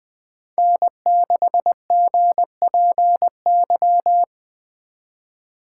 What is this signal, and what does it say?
Morse code, heard as a keyed tone, about 24 wpm, 700 Hz. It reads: N6GPY
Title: Speed Racing - USA 1x3 Call Signs 20wpm